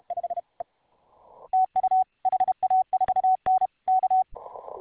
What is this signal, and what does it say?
Morse code, heard as a keyed tone, about 33 wpm, 750 Hz. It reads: TU HA4N K